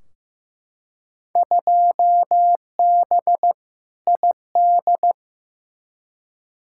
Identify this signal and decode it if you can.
Morse code, heard as a keyed tone, about 16 wpm, 700 Hz. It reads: 2B ID
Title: Field Day Exchanges Encore - 15wpm